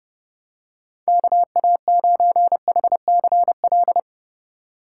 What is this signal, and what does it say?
Morse code, heard as a keyed tone, about 32 wpm, 700 Hz. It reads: KA9HCL